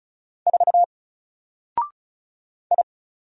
Morse code, heard as a keyed tone, about 36 wpm, 700 Hz. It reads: 4 I
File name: Single-Letter-Number-35wpm